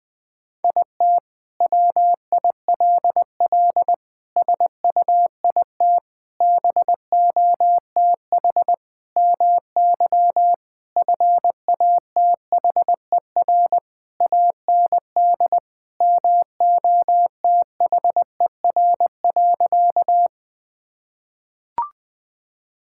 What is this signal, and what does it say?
Morse code, heard as a keyed tone, about 21 wpm, 700 Hz. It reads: IT WILL SUIT BOTH MY FATHER AND MOTHER.
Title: Sentences from Top 1600 Words 20wpm